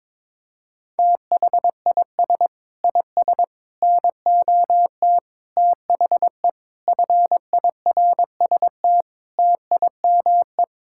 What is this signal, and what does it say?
Morse code, heard as a keyed tone, about 22 wpm, 700 Hz. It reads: THIS IS NOT THE FIRST TIME
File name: practice / 22wpm / Sets of 8 Words - 22wpm